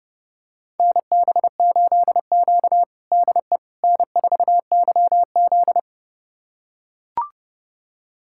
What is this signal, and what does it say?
Morse code, heard as a keyed tone, about 32 wpm, 700 Hz. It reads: NB8Q DE N4YZ
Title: USA Callsign DE Callsign - 4 Characters - 30wpm